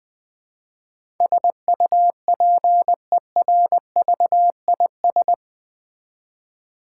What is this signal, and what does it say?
Morse code, heard as a keyed tone, about 20 wpm, 700 Hz. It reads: SUPERVIS
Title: Build A Word - 4 Letters At A Time - 11-Letter Words - 20wpm